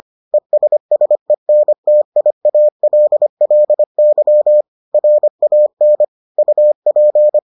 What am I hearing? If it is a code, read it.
Morse code, heard as a keyed tone, about 25 wpm, 600 Hz. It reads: ESSENTIALLY RAN UP